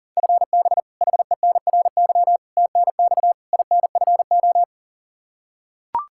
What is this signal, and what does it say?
Morse code, heard as a keyed tone, about 40 wpm, 700 Hz. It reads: FB HENRY TNX INFO
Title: Practice QSO - Exchange 3 - More Info - 40wpm